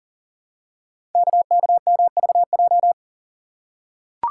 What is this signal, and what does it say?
Morse code, heard as a keyed tone, about 39 wpm, 700 Hz. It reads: KKMVJ
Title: Code Groups - 5 Letters - 40wpm